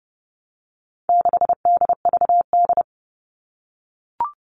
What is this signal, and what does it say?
Morse code, heard as a keyed tone, about 30 wpm, 700 Hz. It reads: 6DVD
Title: All Cut Numbers - 4-Digit 30wpm